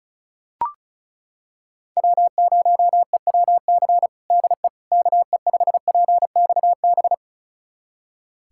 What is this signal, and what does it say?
Morse code, heard as a keyed tone, about 36 wpm, 700 Hz. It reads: W0EWC DE KE5PXB
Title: USA Callsign DE Callsign 35wpm